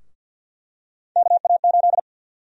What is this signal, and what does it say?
Morse code, heard as a keyed tone, about 52 wpm, 700 Hz. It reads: KA8
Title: USA Call Sign Prefixes - Repeated 2x - 50wpm